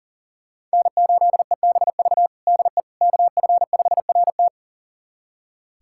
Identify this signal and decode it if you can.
Morse code, heard as a keyed tone, about 40 wpm, 700 Hz. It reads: N8EBV DE KF5RT
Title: USA Callsign DE Callsign - 5 Characters - 40wpm